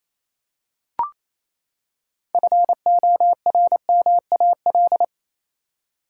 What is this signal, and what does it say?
Morse code, heard as a keyed tone, about 28 wpm, 700 Hz. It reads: FORMAL
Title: Six Letter Words - 28wpm